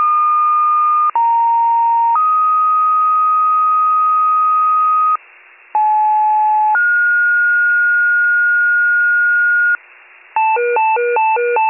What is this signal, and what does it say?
Two-tone page, heard as a one-second tone, then a three-second tone, about 930 then 1250 Hz, about 860 then 1430 Hz, with noise. It is heard from 1.1 to 9.8 s.